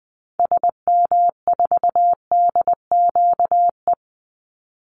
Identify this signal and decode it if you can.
Morse code, heard as a keyed tone, about 20 wpm, 700 Hz. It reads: SM4DQE
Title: Call Signs Repeated 3x - 20wpm